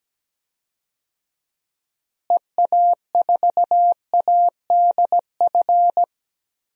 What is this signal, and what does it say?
Morse code, heard as a keyed tone, about 17 wpm, 700 Hz. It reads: EA4ADF
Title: DXCC-100 Call Signs - 6 Characters - Repeated 2x - 17wpm